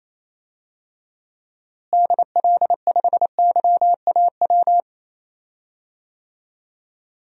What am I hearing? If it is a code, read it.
Morse code, heard as a keyed tone, about 28 wpm, 700 Hz. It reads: DL5YAW
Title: DXCC-100 Call Signs - 6 Characters - Repeated 3x - 28wpm